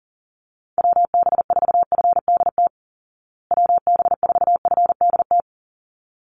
Morse code, heard as a keyed tone, about 37 wpm, 700 Hz. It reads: WB4FDT WB4FDT
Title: Call Signs Repeated 3x - 40wpm